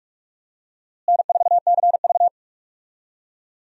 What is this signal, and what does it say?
Morse code, heard as a keyed tone, about 46 wpm, 700 Hz. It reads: N4CV
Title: USA Call Signs - 4 Characters - Repeated 3x - 45wpm